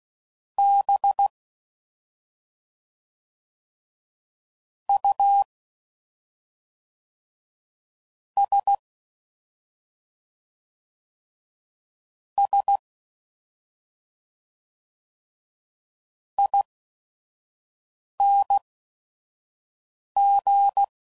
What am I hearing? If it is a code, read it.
Morse code, heard as a keyed tone, about 16 wpm, spaced as for 5 wpm, 800 Hz. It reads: B U S S ING
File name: Suffixes / Special ING's / SlowToFaster ING's